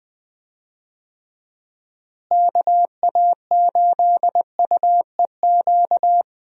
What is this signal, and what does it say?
Morse code, heard as a keyed tone, about 20 wpm, 700 Hz. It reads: KA8UEQ
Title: USA Call Signs - 6 Characters - Repeated 3x - 20wpm